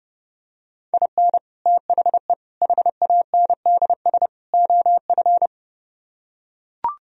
Morse code, heard as a keyed tone, about 30 wpm, 700 Hz. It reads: IN THE HANDS OF